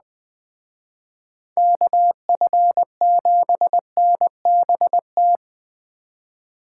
Morse code, heard as a keyed tone, about 20 wpm, 700 Hz. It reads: KF7NBT